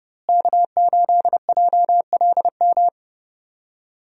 Morse code, heard as a keyed tone, about 30 wpm, 700 Hz. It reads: K8JLM
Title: USA Call Signs - 5 Characters - Repeated 2x - 30wpm